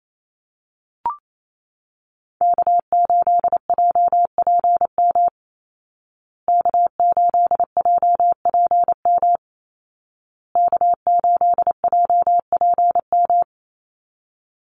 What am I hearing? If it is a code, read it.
Morse code, heard as a keyed tone, about 28 wpm, 700 Hz. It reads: K8JPM K8JPM K8JPM
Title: Call Signs Repeated 2x - 28wpm